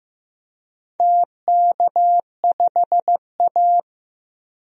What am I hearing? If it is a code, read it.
Morse code, heard as a keyed tone, about 15 wpm, 700 Hz. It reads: TK5A
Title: DXCC-100 Call Signs - 4 Characters - 15wpm